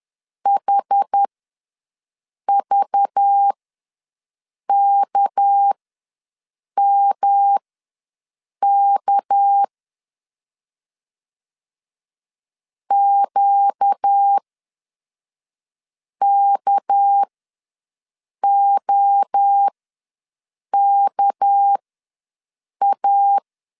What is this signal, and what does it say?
Morse code, heard as a keyed tone, about 10 wpm, spaced as for 5 wpm, 800 Hz. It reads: HVKMK QKOKA
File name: LEZIONI / 6 PKX0 / 6 PK ws 10wpm 3745